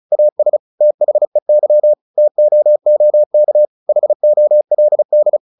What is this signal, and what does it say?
Morse code, heard as a keyed tone, about 34 wpm, 600 Hz. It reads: AS THEY TOOK HOLD
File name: the old man and the sea 2 35wpm